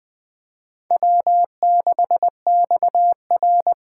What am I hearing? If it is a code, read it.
Morse code, heard as a keyed tone, about 19 wpm, 700 Hz. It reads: W6XR
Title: K1USN SST QSO Practice - 20wpm